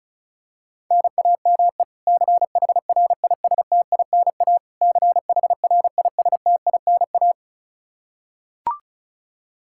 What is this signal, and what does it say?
Morse code, heard as a keyed tone, about 36 wpm, 700 Hz. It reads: NAME CHRISTINA CHRISTINA
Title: K1USN SST QSO Practice 35wpm